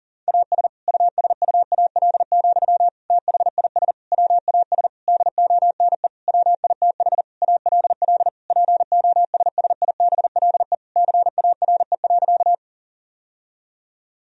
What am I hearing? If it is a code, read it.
Morse code, heard as a keyed tone, about 40 wpm, 700 Hz. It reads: AS USUAL, THIS WAS DONE WITH ALL POSSIBLE CARE.
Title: Sentences from Top 1000 Words - Review Part 2 - 40wpm